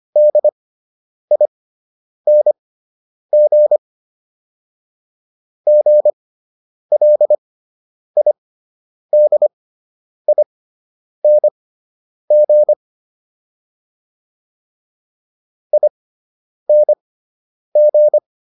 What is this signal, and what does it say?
Morse code, heard as a keyed tone, about 25 wpm, spaced as for 9 wpm, 600 Hz. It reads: DING GLIDING ING